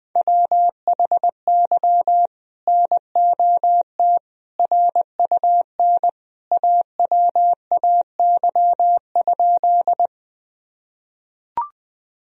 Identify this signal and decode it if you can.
Morse code, heard as a keyed tone, about 21 wpm, 700 Hz. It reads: WHY NOT RUN AWAY?
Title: Sentences from Top 1000 Words - Review Part 1 - 20wpm